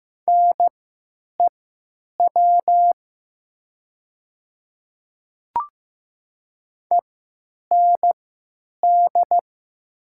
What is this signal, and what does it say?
Morse code, heard as a keyed tone, about 15 wpm, spaced as for 9 wpm, 700 Hz. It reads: NEW END
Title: 3x Character Spacing - Top 200 Words - 15wpm